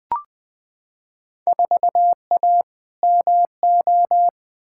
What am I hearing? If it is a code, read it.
Morse code, heard as a keyed tone, about 21 wpm, 700 Hz. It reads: 4A MO
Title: Field Day Exchanges - 20wpm